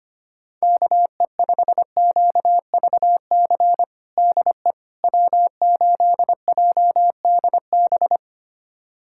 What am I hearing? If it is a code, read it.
Morse code, heard as a keyed tone, about 25 wpm, 700 Hz. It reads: KE5QVC DE W8JDB
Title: USA Callsign DE Callsign 25wpm